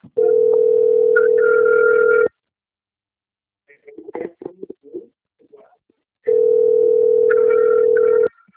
Ringback tone (North America), 440 + 480 Hz.